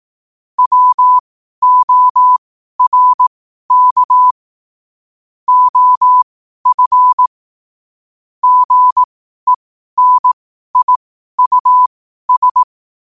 Morse code, heard as a keyed tone, about 18 wpm, spaced as for 12 wpm, 1000 Hz. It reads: WORK OF GENIUS
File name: qotdfiles / 12wpm